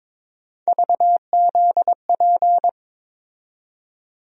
Morse code, heard as a keyed tone, about 22 wpm, 700 Hz. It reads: VZP
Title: USA Call Sign Postfixes - 3 Letters 22wpm